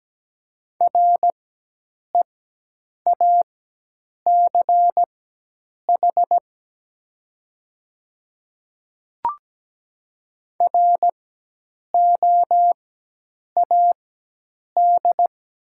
Morse code, heard as a keyed tone, about 17 wpm, spaced as for 8 wpm, 700 Hz. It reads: REACH ROAD
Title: 4x Character Spacing - Top 500 Words - 17wpm